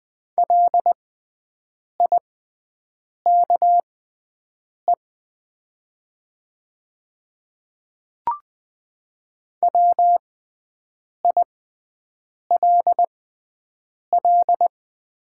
Morse code, heard as a keyed tone, about 20 wpm, spaced as for 7 wpm, 690 Hz. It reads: LIKE WILL
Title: 6x Character Spacing - Top 100 Words - 20wpm